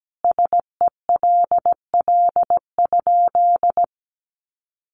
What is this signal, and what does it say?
Morse code, heard as a keyed tone, about 17 wpm, 700 Hz. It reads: SELL?
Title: Sentences from Top 1100 Words 17wpm